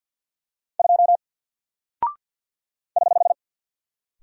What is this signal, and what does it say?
Morse code, heard as a keyed tone, about 48 wpm, 700 Hz. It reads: 2 <HH>
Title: Single Character (Letter:Number:Punctuation:Prosign) 50wpm